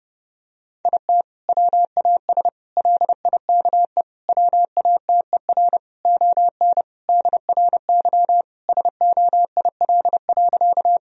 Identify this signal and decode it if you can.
Morse code, heard as a keyed tone, about 32 wpm, 700 Hz. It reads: IT WAS LIKE WATER ON DRY SOIL.
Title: Sentences from Top 1000 Words - Review Part 4 - 30wpm